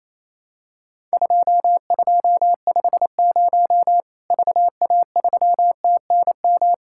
Morse code, heard as a keyed tone, about 28 wpm, 700 Hz. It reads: 2250 VA3TNM
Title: Worked All Europe DX Contest - QTC Exchange - 28wpm